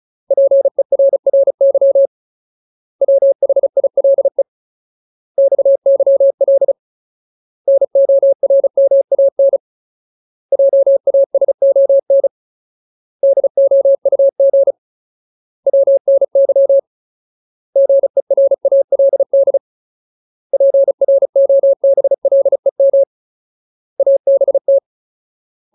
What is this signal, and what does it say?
Morse code, heard as a keyed tone, about 34 wpm, 550 Hz. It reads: PERRY WHILE XYL NORMAN JASON DOUG WNY GERALD PROBLEM ABT
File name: Ultimate Word List / 35WPM Ultimate